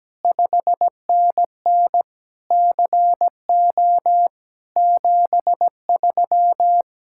Morse code, heard as a keyed tone, about 17 wpm, 700 Hz. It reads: 5NN CO 73
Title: Speed-Racing Common QSO Phrases Encore - 17wpm